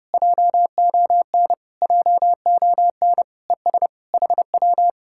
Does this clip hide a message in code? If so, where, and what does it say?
Yes: Morse code from 0.1 to 4.9 s, heard as a keyed tone, about 33 wpm, 700 Hz. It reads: JON JON ES HW